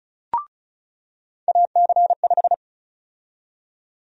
Morse code, heard as a keyed tone, about 36 wpm, 700 Hz. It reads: AC5
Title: USA Call Sign Prefixes - 35wpm